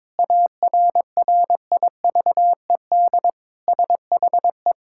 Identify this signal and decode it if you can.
Morse code, heard as a keyed tone, about 22 wpm, 700 Hz. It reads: ARRIVED SHE
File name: practice / 22wpm / Sentences with 3-Word Phrases - 22wpm